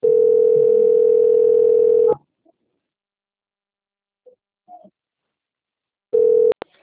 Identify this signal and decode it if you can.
Ringback tone (North America), 440 + 480 Hz.